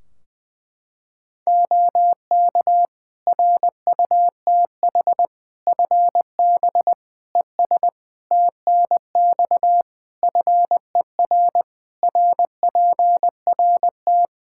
Morse code, heard as a keyed tone, about 20 wpm, 700 Hz. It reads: OK RUTH FB ES TNX FER RPRT
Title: Practice QSO - Exchange 2 - Info - 20wpm